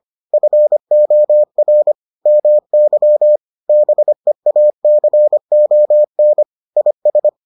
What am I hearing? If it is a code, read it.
Morse code, heard as a keyed tone, about 25 wpm, 600 Hz. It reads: FOR MY BEACON IS